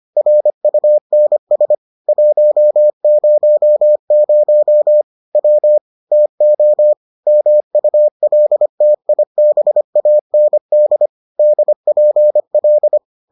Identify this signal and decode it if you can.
Morse code, heard as a keyed tone, about 25 wpm, 600 Hz. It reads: RUNS 100 W TO MULTIBAND DPL